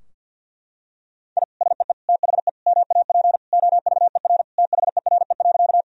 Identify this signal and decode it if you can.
Morse code, heard as a keyed tone, about 46 wpm, 700 Hz. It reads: I SEE THE MAP OVER THERE.
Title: Sentences from Top 1000 Words - Review Part 2 - 50wpm